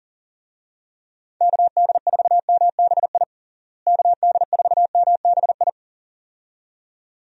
Morse code, heard as a keyed tone, about 40 wpm, 700 Hz. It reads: KD4MBI KD4MBI
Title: USA Call Signs - 6 Characters - Repeated 2x - 40wpm